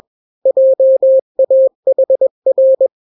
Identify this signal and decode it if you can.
Morse code, heard as a keyed tone, about 21 wpm, 530 Hz. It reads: JAHR
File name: daily / 20wpm